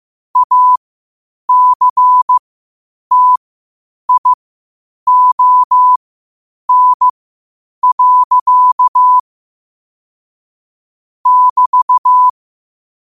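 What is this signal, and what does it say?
Morse code, heard as a keyed tone, about 15 wpm, spaced as for 8 wpm, 1000 Hz. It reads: ACTION. <BT>
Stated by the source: Morse practice at 8 WPM from Morse Resource.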